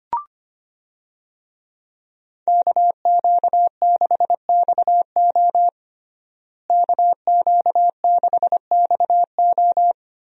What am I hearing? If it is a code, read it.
Morse code, heard as a keyed tone, about 25 wpm, 700 Hz. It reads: KQ6XO KQ6XO
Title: USA Call Signs - 5 Characters - Repeated 2x - 25wpm